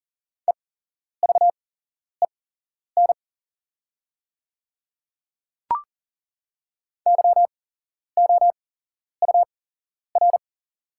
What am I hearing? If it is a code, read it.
Morse code, heard as a keyed tone, about 40 wpm, spaced as for 11 wpm, 690 Hz. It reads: EVEN YOUR
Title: 8x Character Spacing - Top 200 Words - 40wpm